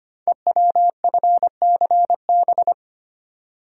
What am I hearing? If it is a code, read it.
Morse code, heard as a keyed tone, about 25 wpm, 690 Hz. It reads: EWFCB
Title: Code Groups - 5 Letters - 25wpm